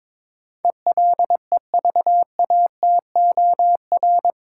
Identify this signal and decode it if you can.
Morse code, heard as a keyed tone, about 22 wpm, 700 Hz. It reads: ELEVATOR
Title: Sentences from Top 1600 Words - New Word Drill - 22wpm